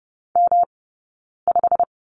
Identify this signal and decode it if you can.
Morse code, heard as a keyed tone, about 30 wpm, spaced as for 9 wpm, 690 Hz. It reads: M5
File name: Lesson 18d - Letter m - Two Characters 30wpm